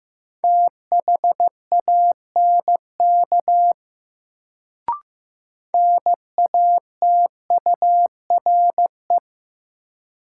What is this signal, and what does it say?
Morse code, heard as a keyed tone, about 15 wpm, 700 Hz. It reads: THANK NATURE
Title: Sentences from Top 600 Words - New Word Drill - 15wpm